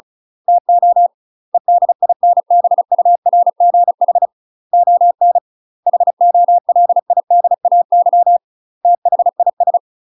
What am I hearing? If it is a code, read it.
Morse code, heard as a keyed tone, about 34 wpm, 700 Hz. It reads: TO EDINBURGH ON HOLIDAY THIS
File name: UK MASTER QSOs TRAINING / 35WPM / UK MASTER QSOs 0030